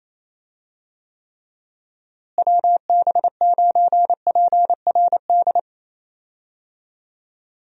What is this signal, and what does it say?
Morse code, heard as a keyed tone, about 28 wpm, 700 Hz. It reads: WB9PRD
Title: USA Call Signs - 6 Characters - Repeated 2x - 28wpm